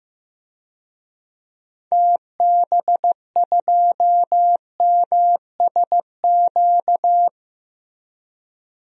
Morse code, heard as a keyed tone, about 15 wpm, 700 Hz. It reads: TB2MSQ
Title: DXCC-100 Call Signs - 6 Characters - Repeated 3x - 15wpm